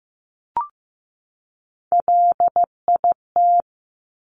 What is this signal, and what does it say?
Morse code, heard as a keyed tone, about 15 wpm, 700 Hz. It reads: LIT
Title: Top Three Letter Combinations in Words 15wpm